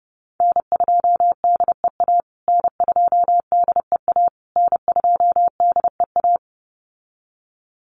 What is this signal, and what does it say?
Morse code, heard as a keyed tone, about 30 wpm, 700 Hz. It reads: N2DEA N2DEA N2DEA
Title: USA Call Signs Repeated 3x 30wpm